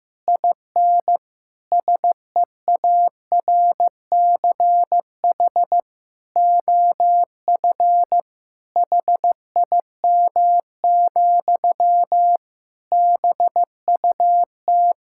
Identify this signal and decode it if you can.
Morse code, heard as a keyed tone, about 16 wpm, 700 Hz. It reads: IN SEARCH OF HIM, BUT
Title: Sentences from Top 1000 Words - Review Part 5 - 15wpm